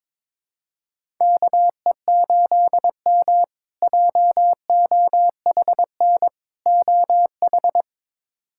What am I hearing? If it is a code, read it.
Morse code, heard as a keyed tone, about 23 wpm, 700 Hz. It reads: KE8M JOHN OH
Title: K1USN SST QSO Practice - 22wpm